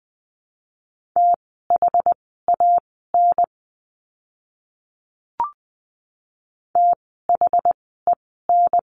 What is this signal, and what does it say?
Morse code, heard as a keyed tone, about 19 wpm, spaced as for 14 wpm, 700 Hz. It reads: THAN THEN
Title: 139 Lesson 23f - Letter f - Words 20wpm